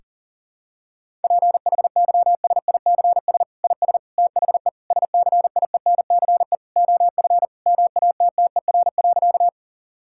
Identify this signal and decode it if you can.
Morse code, heard as a keyed tone, about 40 wpm, 700 Hz. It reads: PHYSICS IS THE SCIENCE OF MATTER.
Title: Sentences from Top 1000 Words - Review Part 5 - 40wpm